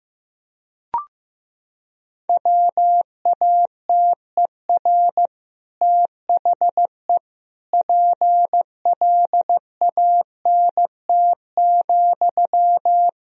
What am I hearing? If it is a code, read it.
Morse code, heard as a keyed tone, about 15 wpm, 700 Hz. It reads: WATER THE PLANT,
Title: Sentences from Top 1000 Words - Review Part 1 - 15wpm